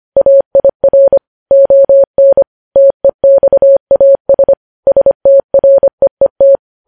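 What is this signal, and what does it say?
Morse code, heard as a keyed tone, about 25 wpm, 550 Hz. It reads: AIR ON TEXAS STREET
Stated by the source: Morse code news headlines sent at 25 wpm.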